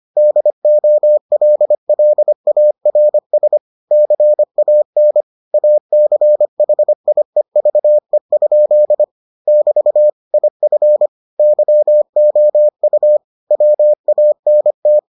Morse code, heard as a keyed tone, about 25 wpm, 600 Hz. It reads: DOLLARS CAN ACHIEVE? <BT> IF YOU WANT